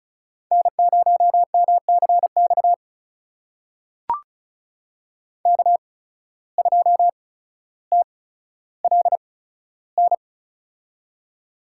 Morse code, heard as a keyed tone, about 34 wpm, spaced as for 15 wpm, 700 Hz. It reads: N0MCX K2TLN
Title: USA Call Signs - 5 Characters - 8x Character Spacing - 35wpm